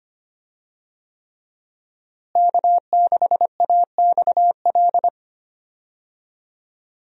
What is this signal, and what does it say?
Morse code, heard as a keyed tone, about 25 wpm, 700 Hz. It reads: K6AXL
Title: USA Call Signs - 5 Characters - Repeated 3x - 25wpm